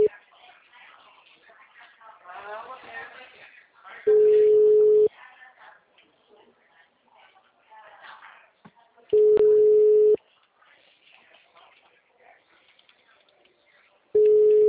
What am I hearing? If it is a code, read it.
Ringback tone (Europe, 425 Hz), with speech.
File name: out- -2006- - - .2